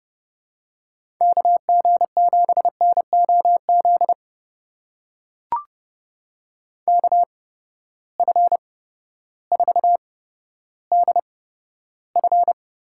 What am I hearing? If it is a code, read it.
Morse code, heard as a keyed tone, about 30 wpm, spaced as for 14 wpm, 700 Hz. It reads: KG7NOZ KF4DF
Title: USA Call Signs - 6 Characters - 8x Character Spacing - 30wpm